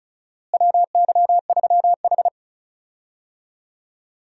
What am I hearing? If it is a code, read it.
Morse code, heard as a keyed tone, about 34 wpm, 700 Hz. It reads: WY3H
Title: USA Call Signs - 4 Characters - Repeated 2x - 35wpm